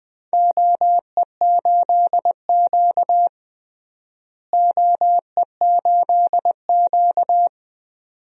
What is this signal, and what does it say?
Morse code, heard as a keyed tone, about 20 wpm, 700 Hz. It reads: OE8Q OE8Q
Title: DXCC-100 Call Signs - 4 Characters - Repeated 2x - 20wpm